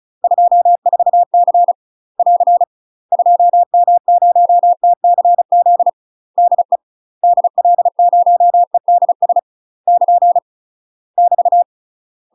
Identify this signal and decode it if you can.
Morse code, heard as a keyed tone, about 34 wpm, 700 Hz. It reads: 24C <AR> 2M0TCZ DE DL0EDS <KN> <BT>
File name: UK MASTER QSOs TRAINING / 35WPM / UK MASTER QSOs 0042